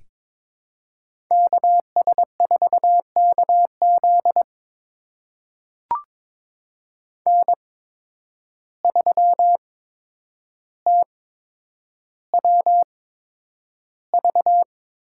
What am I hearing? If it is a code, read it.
Morse code, heard as a keyed tone, about 22 wpm, spaced as for 9 wpm, 700 Hz. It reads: KS4KZ N3TWV
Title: USA Call Signs - 5 Characters - 8x Character Spacing - 22wpm